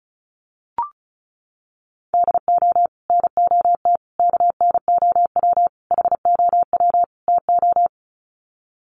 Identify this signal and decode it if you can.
Morse code, heard as a keyed tone, about 34 wpm, 700 Hz. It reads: DO NOT KNOW HOW TO
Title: Sets of 5 Words 35wpm